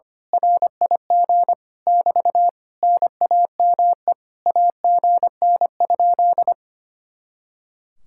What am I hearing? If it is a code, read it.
Morse code, heard as a keyed tone, about 24 wpm, 700 Hz. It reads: RIG <BT> NAME AGN?
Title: Speed-Racing Common QSO Phrases - 25wpm